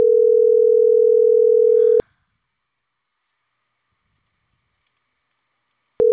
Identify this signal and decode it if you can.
Ringback tone (North America), 440 + 480 Hz.